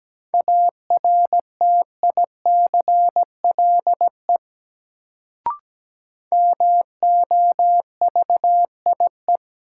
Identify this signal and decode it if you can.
Morse code, heard as a keyed tone, about 17 wpm, 700 Hz. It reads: ARTICLE MOVIE
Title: Sentences from Top 1000 Words - Review Part 5 - New Word Drill - 17wpm